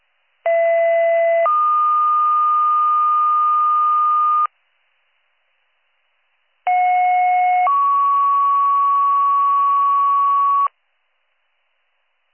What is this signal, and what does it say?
Two-tone page, heard as a one-second tone, then a three-second tone, about 670 then 1220 Hz, about 730 then 1120 Hz. It is heard from 0.4 to 10.7 s.